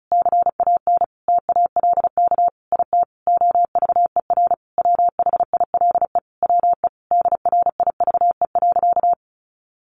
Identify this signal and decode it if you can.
Morse code, heard as a keyed tone, about 34 wpm, 700 Hz. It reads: CAN TALK IT OVER WHILE WE DRIVE.
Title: Sentences from Top 1300 Words 35wpm